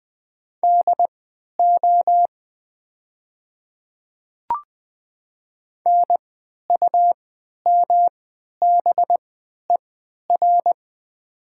Morse code, heard as a keyed tone, about 20 wpm, spaced as for 11 wpm, 700 Hz. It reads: DO NUMBER
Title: 3x Character Spacing - Top 100 Words - 20wpm